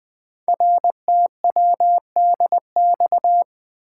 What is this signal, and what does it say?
Morse code, heard as a keyed tone, about 20 wpm, 700 Hz. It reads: RTWDX
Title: Code Groups - 5 Letters - 20wpm